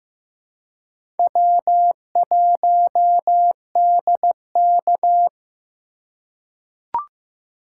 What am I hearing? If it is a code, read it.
Morse code, heard as a keyed tone, about 15 wpm, 700 Hz. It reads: W1DK
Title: USA Call Signs - 2x Character Spacing - 15wpm